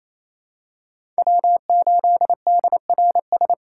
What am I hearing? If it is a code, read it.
Morse code, heard as a keyed tone, about 28 wpm, 700 Hz. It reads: W8DRS
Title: USA Call Signs - 5 Characters - 2x Character Spacing - 28wpm